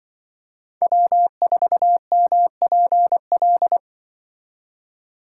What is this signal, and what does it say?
Morse code, heard as a keyed tone, about 25 wpm, 700 Hz. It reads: W4MPL
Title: Speed Racing - USA 1x3 Call Signs 20wpm